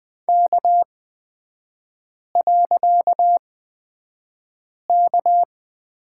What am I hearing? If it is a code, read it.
Morse code, heard as a keyed tone, about 20 wpm, spaced as for 5 wpm, 700 Hz. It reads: K.K
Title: 255 Lesson 41e - Period - Three Characters 20wpm